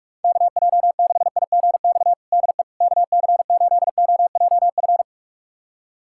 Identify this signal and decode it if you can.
Morse code, heard as a keyed tone, about 47 wpm, 690 Hz. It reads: KJ6IGX DE KC8OJF
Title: USA Callsign DE Callsign - 6 Characters - 45wpm